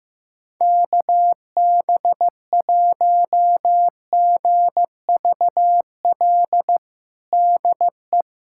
Morse code, heard as a keyed tone, about 15 wpm, 700 Hz. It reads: KB1GVL DE
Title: USA Callsign DE Callsign - 6 Characters - 15wpm